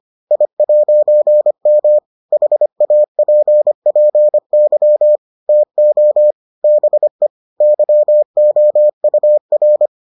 Morse code, heard as a keyed tone, about 25 wpm, 600 Hz. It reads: I'M HAPPY TO BE YOUR